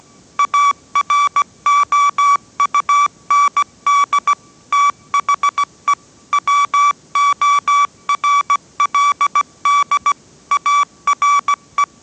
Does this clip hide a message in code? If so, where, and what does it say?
Yes: Morse code from 0.4 to 11.8 s, heard as a keyed tone, about 17 wpm, 1150 Hz. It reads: AROUND THE WORLD ARE